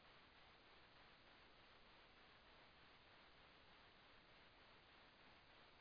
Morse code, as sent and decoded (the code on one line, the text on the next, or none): none